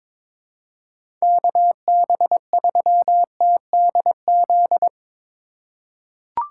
Morse code, heard as a keyed tone, about 22 wpm, 700 Hz. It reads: KB3TDZ
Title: USA Call Signs - 6 Characters - 3x Character Spacing - 22wpm